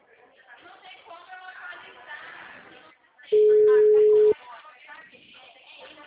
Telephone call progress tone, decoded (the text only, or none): ringback tone